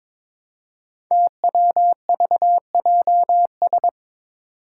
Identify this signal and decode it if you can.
Morse code, heard as a keyed tone, about 22 wpm, 700 Hz. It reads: TWVJS
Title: Five Challenging Letter-Number Combinations 22wpm